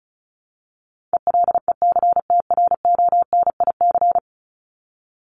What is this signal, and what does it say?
Morse code, heard as a keyed tone, about 34 wpm, 700 Hz. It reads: ELECTRONIC
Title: Top 1001-1250 Words 35wpm